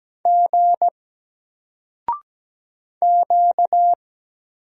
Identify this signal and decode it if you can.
Morse code, heard as a keyed tone, about 17 wpm, 700 Hz. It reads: G Q
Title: Single Character (Letter:Number:Punctuation:Prosign) Alt 17wpm